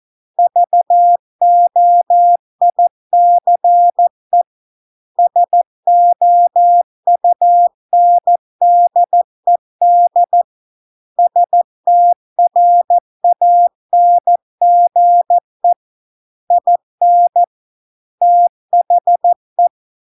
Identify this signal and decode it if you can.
Morse code, heard as a keyed tone, about 14 wpm, 700 Hz. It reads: VOICE SOUNDED STRANGE IN THE